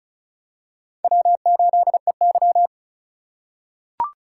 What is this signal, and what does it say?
Morse code, heard as a keyed tone, about 36 wpm, 700 Hz. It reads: W8EY
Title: USA Call Signs - 4 Characters - 4x Character Spacing - 35wpm